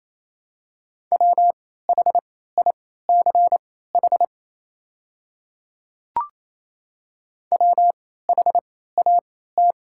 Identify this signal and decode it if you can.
Morse code, heard as a keyed tone, about 28 wpm, spaced as for 16 wpm, 700 Hz. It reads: WHICH WHAT
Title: 3x Character Spacing - Top 100 Words - 28wpm